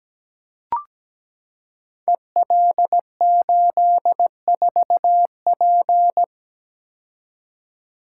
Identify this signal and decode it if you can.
Morse code, heard as a keyed tone, about 17 wpm, 700 Hz. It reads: EL84P
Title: Code Groups - 5 Letter:Numbers - 17wpm